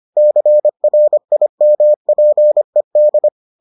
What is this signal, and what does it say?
Morse code, heard as a keyed tone, about 25 wpm, 600 Hz. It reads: CRIMPED